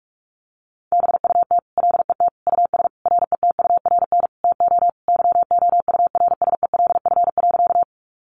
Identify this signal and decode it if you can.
Morse code, heard as a keyed tone, about 44 wpm, 700 Hz. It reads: BUT LET US RETURN TO YOURSELF.
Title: Sentences from Top 800 Words 45wpm